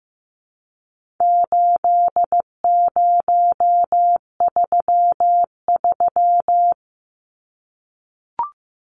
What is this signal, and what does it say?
Morse code, heard as a keyed tone, about 15 wpm, 700 Hz. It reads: 8033
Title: Numbers - 4-Digit 15wpm